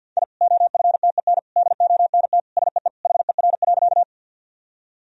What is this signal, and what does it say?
Morse code, heard as a keyed tone, about 51 wpm, 700 Hz. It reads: I OFTEN DONT SEE HER.
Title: Sentences from Top 1000 Words - Review Part 1 - 50wpm